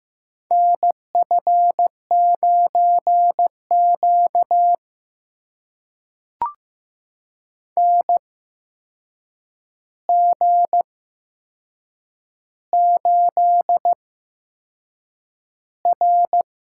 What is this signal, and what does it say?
Morse code, heard as a keyed tone, about 15 wpm, spaced as for 6 wpm, 700 Hz. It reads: NF9Q NG8R
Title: USA Call Signs - 4 Characters - 8x Character Spacing - 15wpm